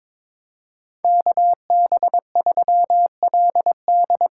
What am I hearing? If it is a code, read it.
Morse code, heard as a keyed tone, about 22 wpm, 700 Hz. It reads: KB3LD
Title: USA Call Signs - 6 Characters - 6x Character Spacing - 22wpm